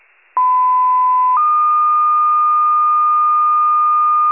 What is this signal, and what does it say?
Two-tone page, heard as a one-second tone, then a three-second tone, about 1010 then 1220 Hz. It is heard from 0.4 to 4.3 s.